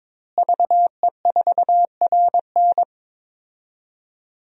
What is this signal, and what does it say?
Morse code, heard as a keyed tone, about 21 wpm, 700 Hz. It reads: VE4RN
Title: DXCC-100 Call Signs - 5 Characters - Repeated 3x - 22wpm